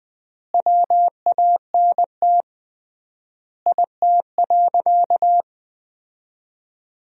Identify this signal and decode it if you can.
Morse code, heard as a keyed tone, about 20 wpm, 700 Hz. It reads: WANT IT.
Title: 3x Word-Spacing - Sentences from Top 300 Words - 20wpm